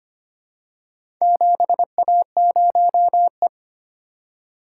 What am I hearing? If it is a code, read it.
Morse code, heard as a keyed tone, about 25 wpm, 700 Hz. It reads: 7A0E
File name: practice / 25wpm / Four Letter-Number Combinations - 25wpm